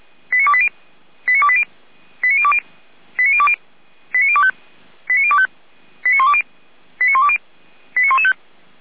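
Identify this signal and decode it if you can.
Five-tone selective call (ZVEI), heard as quick runs of five tones, tones of about 70 ms. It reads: 89280 89280 89020 89020 89025 89025 89120 89120 89115
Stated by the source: ZVEI five-tone selective call received with an SDR, logged by Feuerwehr Orlamünde.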